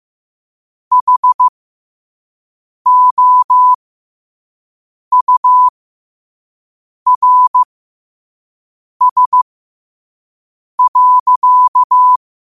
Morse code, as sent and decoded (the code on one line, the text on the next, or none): .... --- ..- .-. ... .-.-.-
HOURS.